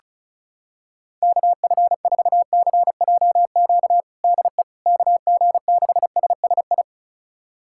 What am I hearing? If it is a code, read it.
Morse code, heard as a keyed tone, about 36 wpm, 700 Hz. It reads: KF4CJQ DE KG6SSI
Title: USA Callsign DE Callsign - 6 Characters - 35wpm